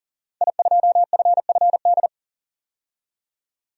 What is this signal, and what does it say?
Morse code, heard as a keyed tone, about 40 wpm, 700 Hz. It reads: I2FFD
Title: DXCC-100 Call Signs - 5 Characters - 40wpm